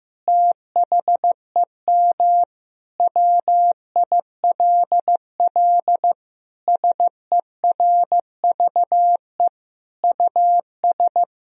Morse code, heard as a keyed tone, about 15 wpm, 700 Hz. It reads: THEM WILL SERVE US
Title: Sentences from Top 1000 Words - Review Part 4 - 15wpm